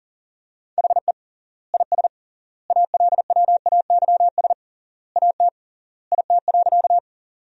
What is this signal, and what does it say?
Morse code, heard as a keyed tone, about 40 wpm, 700 Hz. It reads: HE IS ALWAYS AT IT.
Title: 3x Word-Spacing - Sentences from Top 300 Words - 40wpm